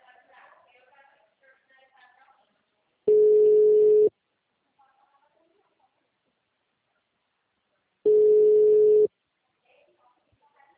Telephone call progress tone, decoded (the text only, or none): ringback tone